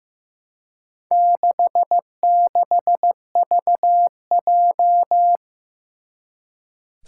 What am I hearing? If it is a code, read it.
Morse code, heard as a keyed tone, about 15 wpm, 700 Hz. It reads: 66VJ
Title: Four Challenging Letter-Number Combinations 15wpm